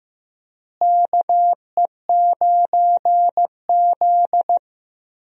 Morse code, heard as a keyed tone, about 15 wpm, 700 Hz. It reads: KE9Z
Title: USA Call Signs - 4 Characters - Repeated 2x - 15wpm